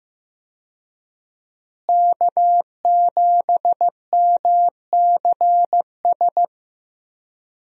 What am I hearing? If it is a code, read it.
Morse code, heard as a keyed tone, about 15 wpm, 700 Hz. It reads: K7MCS
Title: USA Call Signs - 5 Characters - Repeated 3x - 15wpm